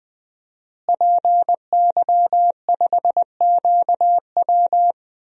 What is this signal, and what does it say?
Morse code, heard as a keyed tone, about 20 wpm, 700 Hz. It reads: PY5QW
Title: DXCC-100 Call Signs - 5 Characters - Repeated 3x - 20wpm